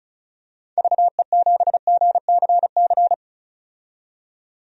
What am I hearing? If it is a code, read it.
Morse code, heard as a keyed tone, about 36 wpm, 700 Hz. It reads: VE7GCC
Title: DXCC-100 Call Signs - 6 Characters - 35wpm